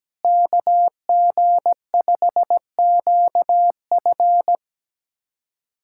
Morse code, heard as a keyed tone, about 17 wpm, 700 Hz. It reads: KG5QF
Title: USA Callsign DE Callsign 17wpm